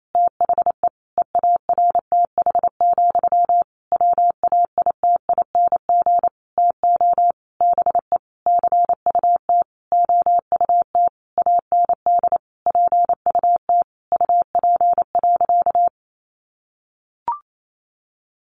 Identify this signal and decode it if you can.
Morse code, heard as a keyed tone, about 28 wpm, 700 Hz. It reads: THE EARTH, WAITING TO BE CUT OUT AND PUT UP.
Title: A Sound Of Thunder 28wpm